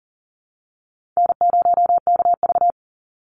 Morse code, heard as a keyed tone, about 39 wpm, 700 Hz. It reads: N0KV
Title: Call Signs Repeated 3x - 40wpm